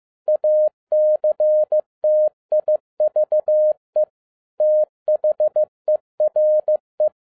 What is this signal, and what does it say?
Morse code, heard as a keyed tone, about 15 wpm, 610 Hz. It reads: ACTIVE THERE
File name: FIELD DAY CONTEST QRV FROM SAKAE VILLAGE 15wpm